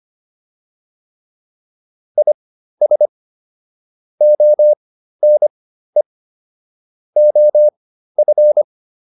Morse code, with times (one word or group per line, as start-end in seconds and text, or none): 2.2-3.1 IS
4.2-6.0 ONE
7.2-8.6 OF